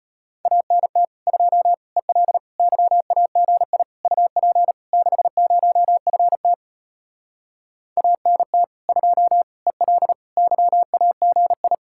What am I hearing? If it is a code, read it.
Morse code, heard as a keyed tone, about 33 wpm, 700 Hz. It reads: ANT 2 EL YAGI UP 60FT ANT 2 EL YAGI